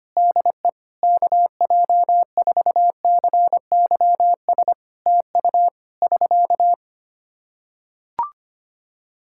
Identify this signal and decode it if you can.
Morse code, heard as a keyed tone, about 25 wpm, 700 Hz. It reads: DE KJ4CYS TU <SK>
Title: Practice QSO - Exchange 4 - Ending - 25wpm